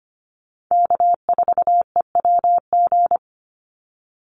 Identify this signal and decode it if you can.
Morse code, heard as a keyed tone, about 25 wpm, 700 Hz. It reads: K4EWG
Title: Call Signs Repeated 3x - 25wpm